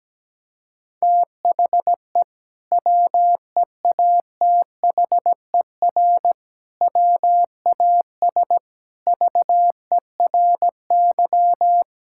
Morse code, heard as a keyed tone, about 17 wpm, 700 Hz. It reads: THE WEATHER WAS VERY
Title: Sentences from Top 1000 Words - Review Part 5 - 17wpm